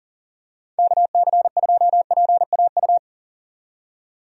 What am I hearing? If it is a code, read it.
Morse code, heard as a keyed tone, about 40 wpm, 700 Hz. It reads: KC2PAU
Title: USA Call Signs - 6 Characters - 40wpm